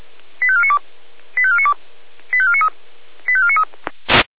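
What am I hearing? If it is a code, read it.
Five-tone selective call (ZVEI), heard as quick runs of five tones, tones of about 70 ms. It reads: 85492 85492 85493 85493
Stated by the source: ZVEI five-tone selective call received by SDR.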